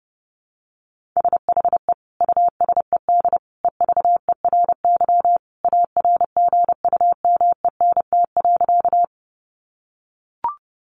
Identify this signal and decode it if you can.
Morse code, heard as a keyed tone, about 30 wpm, 700 Hz. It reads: SHE USED EVERY ARGUMENT.
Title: Sentences from Top 1400 Words 30wpm